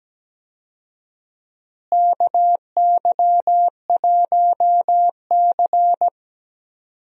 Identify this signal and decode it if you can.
Morse code, heard as a keyed tone, about 17 wpm, 700 Hz. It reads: KY1C
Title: USA Call Signs - 4 Characters - Repeated 2x - 17wpm